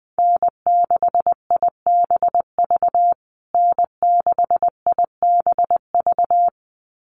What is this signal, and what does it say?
Morse code, heard as a keyed tone, about 20 wpm, 700 Hz. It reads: N6IBV N6IBV
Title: USA Call Signs Repeated 2x 20wpm